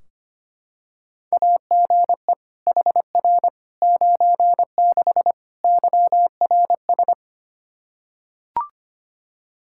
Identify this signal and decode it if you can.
Morse code, heard as a keyed tone, about 25 wpm, 700 Hz. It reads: AGE HR 96 YRS
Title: K1USN SST QSO Practice 25wpm